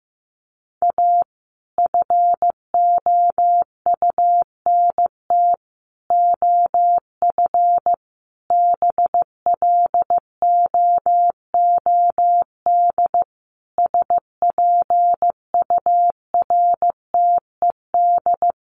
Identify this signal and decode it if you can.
Morse code, heard as a keyed tone, about 15 wpm, 700 Hz. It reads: A FOUNT OF BLOOD SPURTED
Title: A Sound Of Thunder 15wpm